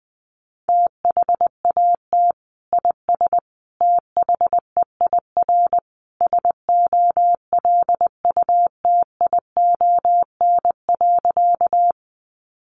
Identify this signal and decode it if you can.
Morse code, heard as a keyed tone, about 20 wpm, 700 Hz. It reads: THAT IS THEIR SOLUTION.
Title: Sentences from Top 1300 Words 20wpm